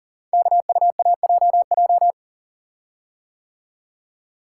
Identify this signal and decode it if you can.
Morse code, heard as a keyed tone, about 39 wpm, 700 Hz. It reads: KUAJJ
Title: Code Groups - 5 Letters - 40wpm